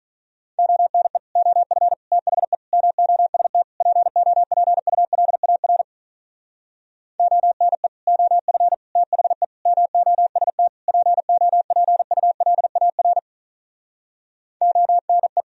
Morse code, heard as a keyed tone, about 44 wpm, 700 Hz. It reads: ONE OF THE MOST POPULAR ONE OF THE MOST POPULAR ONE
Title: Speed-Racing - Sets of 5 Words 35wpm